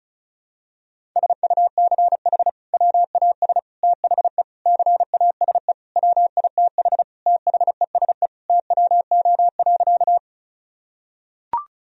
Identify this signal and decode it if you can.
Morse code, heard as a keyed tone, about 36 wpm, 700 Hz. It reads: SUCH WAS THE CASE WITH THESE TWO.
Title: Sentences from Top 1000 Words - Review Part 1 - 35wpm